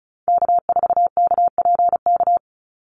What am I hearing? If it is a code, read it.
Morse code, heard as a keyed tone, about 34 wpm, 700 Hz. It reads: K4KPK
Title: NA SOTA Activator Callsigns 35wpm